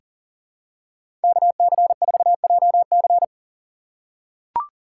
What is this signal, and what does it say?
Morse code, heard as a keyed tone, about 40 wpm, 700 Hz. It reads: KC4JC
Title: USA Call Signs - 5 Characters - 2x Character Spacing - 40wpm